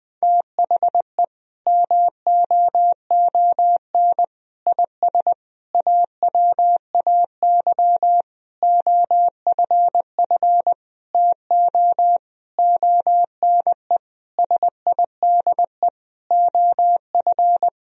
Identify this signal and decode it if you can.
Morse code, heard as a keyed tone, about 20 wpm, 700 Hz. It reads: THE MOON IS AWAY OFF TO ONE SIDE OF
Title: Sentences from Top 1000 Words - Review Part 3 - 20wpm